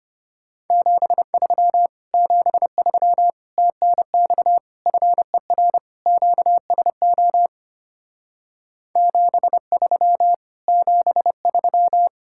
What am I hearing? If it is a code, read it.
Morse code, heard as a keyed tone, about 33 wpm, 690 Hz. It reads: 73 73 TNX FER QSO 73 73